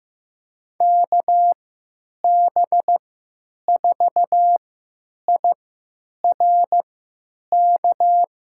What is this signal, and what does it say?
Morse code, heard as a keyed tone, about 15 wpm, spaced as for 9 wpm, 700 Hz. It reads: KB4IRK
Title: USA Call Signs - 6 Characters - 3x Character Spacing - 15wpm